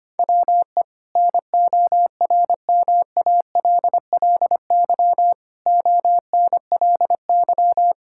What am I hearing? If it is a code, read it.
Morse code, heard as a keyed tone, about 25 wpm, 700 Hz. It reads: WE NORMALLY ONLY
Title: Sentences from Top 1000 Words - Review Part 6 - 25wpm